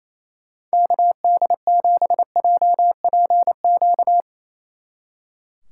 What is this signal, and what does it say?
Morse code, heard as a keyed tone, about 36 wpm, 700 Hz. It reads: KD7JPQ